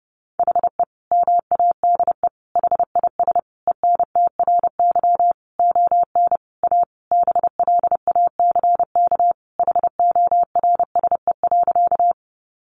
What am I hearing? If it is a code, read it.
Morse code, heard as a keyed tone, about 30 wpm, 700 Hz. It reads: HE MADE HIS ENTRY ON A BLACK HORSE.
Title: Sentences from Top 1500 Words 30wpm